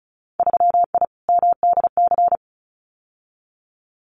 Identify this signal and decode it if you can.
Morse code, heard as a keyed tone, about 36 wpm, 700 Hz. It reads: 3I MDC
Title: Winter Field Day Exchanges Encore 35wpm